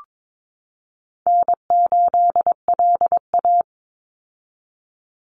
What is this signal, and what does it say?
Morse code, heard as a keyed tone, about 22 wpm, 700 Hz. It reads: N8LA
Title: Easy Call Signs - 22wpm